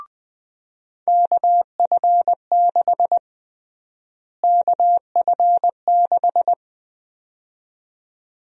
Morse code, heard as a keyed tone, about 20 wpm, 700 Hz. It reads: KF6 KF6
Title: USA Call Sign Prefixes - Repeated 2x - 20wpm